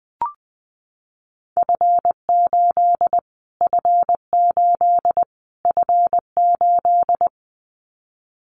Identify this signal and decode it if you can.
Morse code, heard as a keyed tone, about 20 wpm, 700 Hz. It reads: F8 F8 F8
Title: Callsign Prefixes for Top 25 DXCC - Repeated 3x - 20wpm